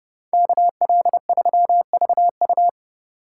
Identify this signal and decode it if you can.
Morse code, heard as a keyed tone, about 36 wpm, 700 Hz. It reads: KL3VU
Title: Speed Racing - USA 2x2 Call Signs 30wpm